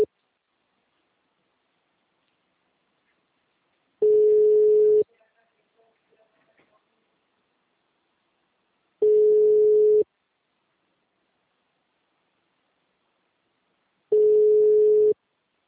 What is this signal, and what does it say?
Ringback tone (Europe, 425 Hz).